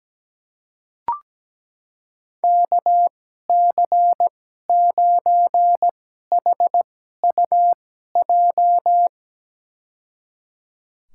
Morse code, heard as a keyed tone, about 17 wpm, spaced as for 12 wpm, 700 Hz. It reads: KC9HUJ
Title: USA Call Signs - 2x Character Spacing - 17wpm